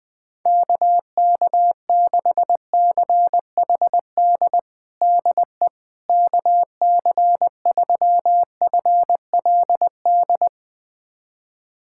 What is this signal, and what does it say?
Morse code, heard as a keyed tone, about 20 wpm, 700 Hz. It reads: KK6CHD DE KC3FLD
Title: USA Callsign DE Callsign - 6 Characters - 20wpm